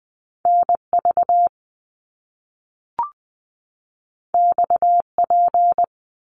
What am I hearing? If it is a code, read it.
Morse code, heard as a keyed tone, about 20 wpm, 700 Hz. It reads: N V X P
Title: Top Two Letter Combinations in Words 20wpm